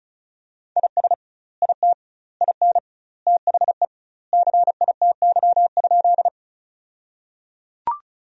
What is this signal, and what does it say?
Morse code, heard as a keyed tone, about 34 wpm, 700 Hz. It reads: IS IT IN THE CITY?
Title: 2x Word-Spacing - Sentences from Top 300 Words - 35wpm